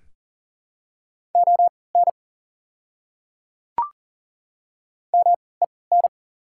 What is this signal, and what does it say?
Morse code, heard as a keyed tone, about 40 wpm, spaced as for 23 wpm, 690 Hz. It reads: ON MEN
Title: 3x Character Spacing - Top 200 Words - 40wpm